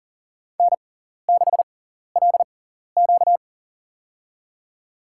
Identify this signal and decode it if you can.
Morse code, heard as a keyed tone, about 40 wpm, spaced as for 14 wpm, 700 Hz. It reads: N6LQ
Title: USA Call Signs - 4 Characters - 6x Character Spacing - 40wpm